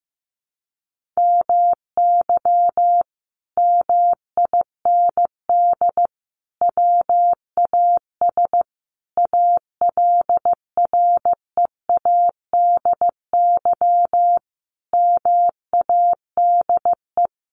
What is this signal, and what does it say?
Morse code, heard as a keyed tone, about 15 wpm, 700 Hz. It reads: MY MIND WAS ALREADY MADE
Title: Sentences from Top 700 Words 15wpm